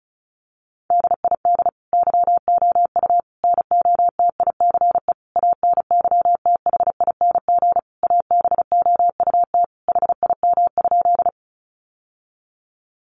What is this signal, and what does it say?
Morse code, heard as a keyed tone, about 34 wpm, 700 Hz. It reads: DID YOU NOTICE ANYTHING ABOUT HIM?
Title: Sentences from Top 1000 Words 35wpm